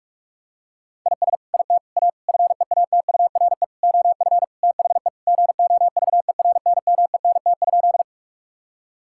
Morse code, heard as a keyed tone, about 42 wpm, 700 Hz. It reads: IS IT A FEATURE OF THE GOVERNMENT?
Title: Sentences from Top 1000 Words - Review Part 5 - 45wpm